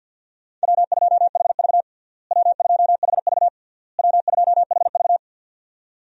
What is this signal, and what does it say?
Morse code, heard as a keyed tone, about 52 wpm, 700 Hz. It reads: W2HV W2HV W2HV
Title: USA Call Signs - 4 Characters - Repeated 3x - 50wpm